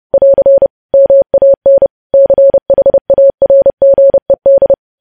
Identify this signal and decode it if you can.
Morse code, heard as a keyed tone, about 30 wpm, 550 Hz. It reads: <AR> MAN CHARGED
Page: news headlines in morse code at 30 wpm